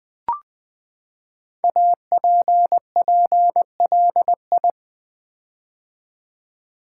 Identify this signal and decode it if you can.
Morse code, heard as a keyed tone, about 20 wpm, 700 Hz. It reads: APPLI
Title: Build A Word - 5 Letters At A Time - 10-Letter Words - 20wpm